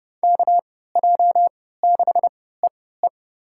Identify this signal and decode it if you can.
Morse code, heard as a keyed tone, about 30 wpm, spaced as for 17 wpm, 700 Hz. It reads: KJ6EE
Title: USA Call Signs - 5 Characters - 3x Character Spacing - 30wpm